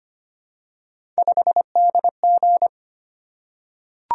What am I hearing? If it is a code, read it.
Morse code, heard as a keyed tone, about 25 wpm, 700 Hz. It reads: 5DG
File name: Three Letter-Number Combinations - 25wpm